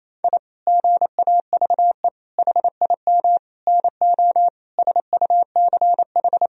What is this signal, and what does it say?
Morse code, heard as a keyed tone, about 28 wpm, 700 Hz. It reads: I GAVE HIM NO SUCH
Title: Sentences from Top 1000 Words - Review Part 2 - 28wpm